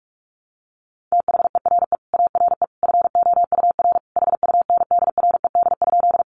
Morse code, heard as a keyed tone, about 44 wpm, 700 Hz. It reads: THERE ARE FOUR HUNDRED?
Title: Sentences from Top 400 Words 45wpm